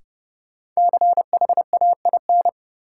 Morse code, heard as a keyed tone, about 32 wpm, 700 Hz. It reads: CHAIN
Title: Sentences from Top 1500 Words - New Word Drill - 30wpm